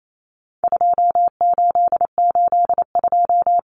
Morse code, heard as a keyed tone, about 28 wpm, 700 Hz. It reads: 2882
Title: Common Cut Numbers - 4-Digit 28wpm